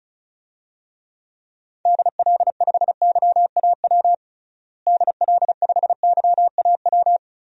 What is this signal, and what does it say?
Morse code, heard as a keyed tone, about 36 wpm, 700 Hz. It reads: DL5YAW DL5YAW
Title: DXCC-100 Call Signs - 6 Characters - Repeated 3x - 35wpm